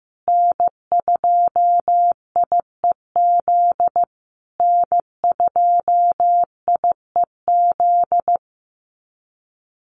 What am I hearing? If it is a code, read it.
Morse code, heard as a keyed tone, about 14 wpm, 700 Hz. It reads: N2IEZ N2IEZ
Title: USA Call Signs Repeated 3x 15wpm